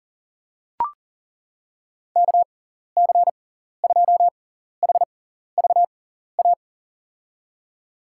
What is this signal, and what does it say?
Morse code, heard as a keyed tone, about 40 wpm, spaced as for 14 wpm, 700 Hz. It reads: KC2HVA
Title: USA Call Signs - 6x Character Spacing - 40wpm